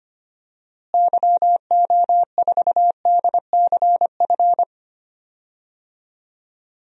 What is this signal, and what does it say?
Morse code, heard as a keyed tone, about 25 wpm, 700 Hz. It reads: YO4DCF
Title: DXCC-100 Call Signs - 6 Characters - Repeated 2x - 25wpm